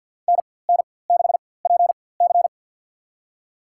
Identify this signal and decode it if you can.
Morse code, heard as a keyed tone, about 51 wpm, spaced as for 23 wpm, 700 Hz. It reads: NN6PC
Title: USA Call Signs - 5 Characters - 4x Character Spacing - 50wpm